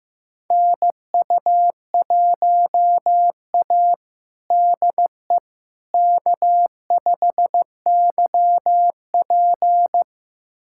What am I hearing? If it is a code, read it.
Morse code, heard as a keyed tone, about 15 wpm, 700 Hz. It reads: NU1A DE K5YP
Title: USA Callsign DE Callsign - 4 Characters - 15wpm